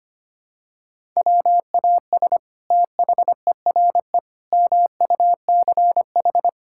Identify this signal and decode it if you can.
Morse code, heard as a keyed tone, about 25 wpm, 700 Hz. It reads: WAS THERE MUCH
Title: Sentences from Top 1000 Words - Review Part 5 - 25wpm